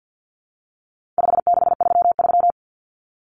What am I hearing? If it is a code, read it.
Morse code, heard as a keyed tone, about 50 wpm, 700 Hz. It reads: 5633
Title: Numbers - 4-Digit 50wpm